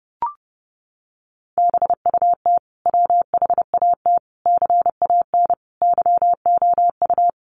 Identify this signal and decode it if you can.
Morse code, heard as a keyed tone, about 30 wpm, 700 Hz. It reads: BUT WHAT CAN YOU
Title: Sentences from Top 1000 Words 30wpm